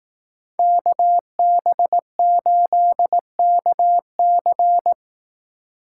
Morse code, heard as a keyed tone, about 18 wpm, 700 Hz. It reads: KB8KC
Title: Speed Racing - USA 2x2 Call Signs 15wpm